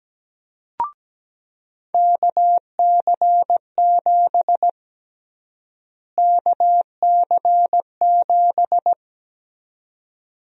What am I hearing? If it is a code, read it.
Morse code, heard as a keyed tone, about 17 wpm, 700 Hz. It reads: KC7 KC7
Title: USA Call Sign Prefixes - Repeated 2x - 17wpm